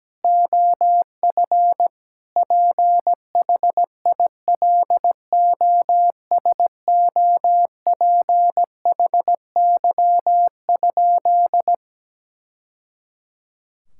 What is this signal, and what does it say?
Morse code, heard as a keyed tone, about 17 wpm, 700 Hz. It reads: OF PHILOSOPHY?
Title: Sentences from Top 1000 Words - Review Part 6 - 17wpm